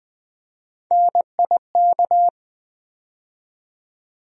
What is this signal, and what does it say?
Morse code, heard as a keyed tone, about 20 wpm, 700 Hz. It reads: NIK
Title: USA Call Sign Postfixes - 3 Letters 20wpm